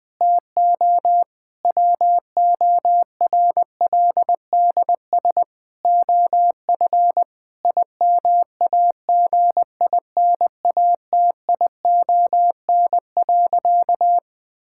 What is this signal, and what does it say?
Morse code, heard as a keyed tone, about 20 wpm, 700 Hz. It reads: TO WORLDS OF IMAGINATION.
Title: Difficult Sentences with Hints - 20wpm